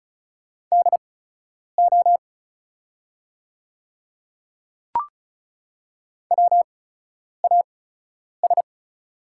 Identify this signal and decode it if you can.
Morse code, heard as a keyed tone, about 36 wpm, spaced as for 10 wpm, 690 Hz. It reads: DO WAS
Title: 8x Character Spacing - Top 200 Words - 35wpm